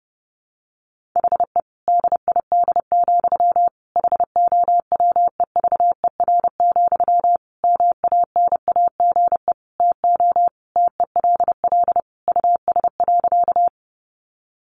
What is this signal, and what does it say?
Morse code, heard as a keyed tone, about 30 wpm, 700 Hz. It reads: HE DID, HOWEVER, MANAGE TO TELL US.
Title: Sentences from Top 1400 Words 30wpm